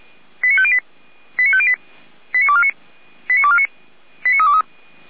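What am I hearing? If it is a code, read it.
Five-tone selective call (ZVEI), heard as quick runs of five tones, tones of about 70 ms. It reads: 89508 89508 89249 89249 89323
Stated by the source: ZVEI five-tone selective call received by SDR, from the Feuerwehr Orlamünde (fire brigade) audio archive.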